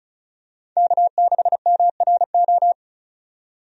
Morse code, heard as a keyed tone, about 36 wpm, 700 Hz. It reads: K6MRO
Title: USA Call Signs - 5 Characters - 35wpm